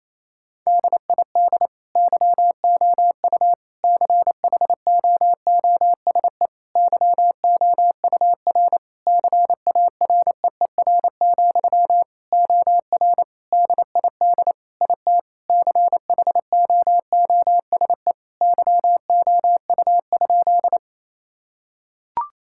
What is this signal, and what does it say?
Morse code, heard as a keyed tone, about 28 wpm, 700 Hz. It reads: DID YOU CHOOSE YOUR CAREER, OR DID IT CHOOSE YOU?
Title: Sentences from Top 1000 Words - Review Part 6 - 28wpm